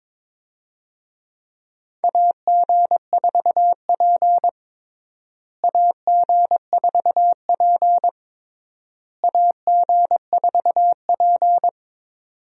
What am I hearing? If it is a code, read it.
Morse code, heard as a keyed tone, about 22 wpm, 700 Hz. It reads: AG4P AG4P AG4P
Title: USA Call Signs - 4 Characters - Repeated 3x - 22wpm